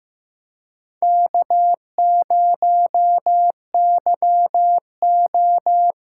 Morse code, heard as a keyed tone, about 15 wpm, 700 Hz. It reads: K0YO
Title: USA Callsign DE Callsign - 4 Characters - 15wpm